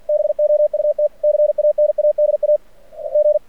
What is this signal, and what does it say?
Morse code, heard as a keyed tone, about 46 wpm, 600 Hz. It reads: BOUT CANADA G
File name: archived / CRSnet